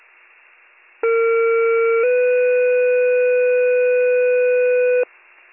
Two-tone page, heard as a one-second tone, then a three-second tone, about 460 then 500 Hz, with noise. It is heard from 1.0 to 5.0 s.